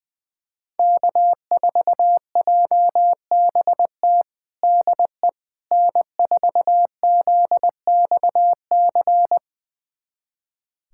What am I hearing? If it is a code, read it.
Morse code, heard as a keyed tone, about 20 wpm, 700 Hz. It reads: K4JBT DE N4ZXC
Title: USA Callsign DE Callsign - 5 Characters - 20wpm